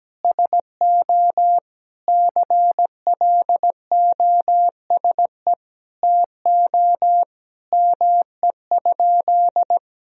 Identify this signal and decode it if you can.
Morse code, heard as a keyed tone, about 17 wpm, 700 Hz. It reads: SO CLOSE TO ME?
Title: Sentences from Top 1000 Words - Review Part 1 - 17wpm